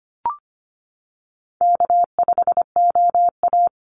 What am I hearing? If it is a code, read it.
Morse code, heard as a keyed tone, about 25 wpm, 700 Hz. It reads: K5OA
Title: Call Signs - 25wpm